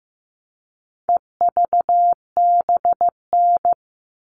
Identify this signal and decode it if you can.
Morse code, heard as a keyed tone, about 15 wpm, 700 Hz. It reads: EVBN
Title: All Cut Numbers - 4-Digit 15wpm